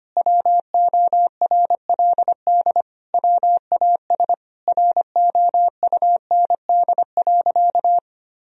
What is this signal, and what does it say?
Morse code, heard as a keyed tone, about 25 wpm, 700 Hz. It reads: WORLD WAS ROUND.
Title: Sentences from Top 1000 Words - Review Part 1 - 25wpm